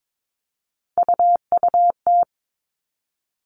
Morse code, heard as a keyed tone, about 22 wpm, 700 Hz. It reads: UUT
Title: All Cut Numbers - 3-Digit 22wpm